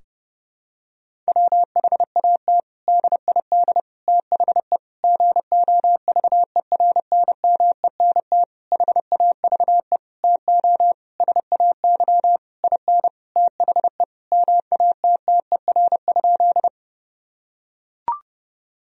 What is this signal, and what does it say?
Morse code, heard as a keyed tone, about 32 wpm, 700 Hz. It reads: WHAT DID THE GOVERNMENT HAVE TO SAY IN THE MATTER?
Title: Sentences from Top 1000 Words - Review Part 3 - 30wpm